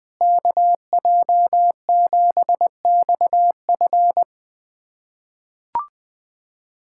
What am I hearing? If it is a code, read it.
Morse code, heard as a keyed tone, about 20 wpm, 700 Hz. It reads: KJ7XF
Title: USA Callsign DE Callsign - 5 Characters - 20wpm